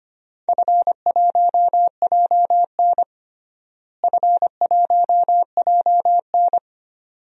Morse code, heard as a keyed tone, about 25 wpm, 700 Hz. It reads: F1JN F1JN
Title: DXCC-100 Call Signs - 4 Characters - Repeated 2x - 25wpm